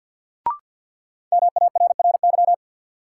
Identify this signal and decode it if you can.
Morse code, heard as a keyed tone, about 47 wpm, 700 Hz. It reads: MARRY
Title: Sentences from Top 1200 Words - New Word Drill - 50wpm